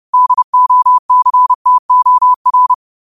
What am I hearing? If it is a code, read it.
Morse code, heard as a keyed tone, about 30 wpm, 1000 Hz. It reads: DOCTOR
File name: qotdfiles / 30wpm